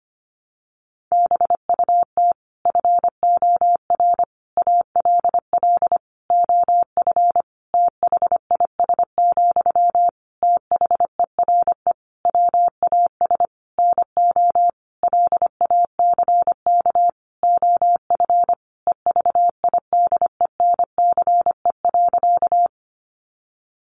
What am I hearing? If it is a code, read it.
Morse code, heard as a keyed tone, about 25 wpm, 700 Hz. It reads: BUT FOR ALL OF THIS, THERE WAS NO LACK OF EVIDENCE.
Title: Sentences from Top 1300 Words 25wpm